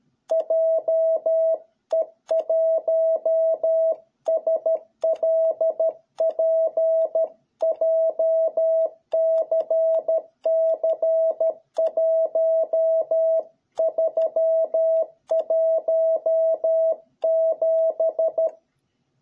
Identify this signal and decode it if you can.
Morse code, heard as a keyed tone, about 13 wpm, 660 Hz. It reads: JE1SLP JCC1317